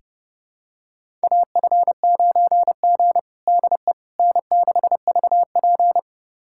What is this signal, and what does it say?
Morse code, heard as a keyed tone, about 30 wpm, 700 Hz. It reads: AF9G DE N6VP
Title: USA Callsign DE Callsign - 4 Characters - 30wpm